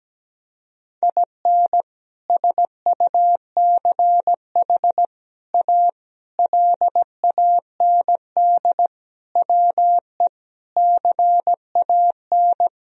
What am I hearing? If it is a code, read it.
Morse code, heard as a keyed tone, about 17 wpm, 700 Hz. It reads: IN SUCH A LAND WE CAN
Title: Sentences from Top 1000 Words - Review Part 1 - 17wpm